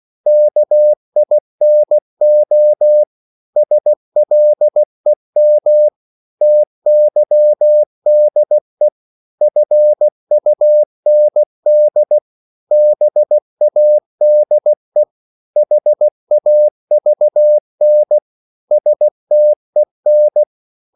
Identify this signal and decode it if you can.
Morse code, heard as a keyed tone, about 16 wpm, 600 Hz. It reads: KINO SLEM TYDE FUND BADE HAVN STEN
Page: Korte danske ord 16wpm